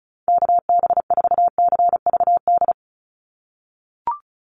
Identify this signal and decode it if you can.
Morse code, heard as a keyed tone, about 36 wpm, 700 Hz. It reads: KB4CVD
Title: USA Call Signs 35wpm